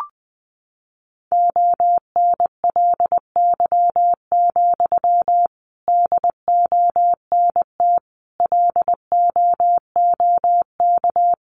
Morse code, heard as a keyed tone, about 20 wpm, 700 Hz. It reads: ONLY, DONT LOOK
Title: Sentences from Top 300 Words 20wpm